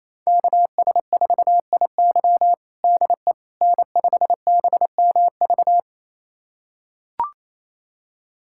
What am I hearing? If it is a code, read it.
Morse code, heard as a keyed tone, about 28 wpm, 700 Hz. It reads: KS4IY DE N5BMV
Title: USA Callsign DE Callsign - 5 Characters - 28wpm